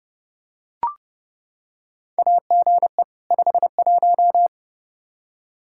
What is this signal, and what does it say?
Morse code, heard as a keyed tone, about 32 wpm, 700 Hz. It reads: AGE 51
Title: Practice QSO - Exchange 3 - More Info - 30wpm